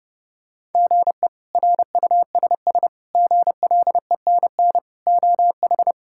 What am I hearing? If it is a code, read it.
Morse code, heard as a keyed tone, about 30 wpm, 700 Hz. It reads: GE RUSS GLENN OH
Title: K1USN SST QSO Practice - 30wpm